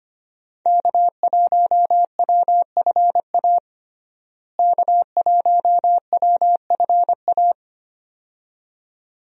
Morse code, heard as a keyed tone, about 25 wpm, 700 Hz. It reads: K1WFA K1WFA
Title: USA Call Signs - 5 Characters - Repeated 2x - 25wpm